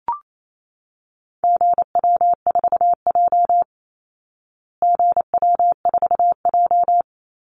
Morse code, heard as a keyed tone, about 28 wpm, 700 Hz. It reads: GW4J GW4J
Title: Call Signs Repeated 2x - 28wpm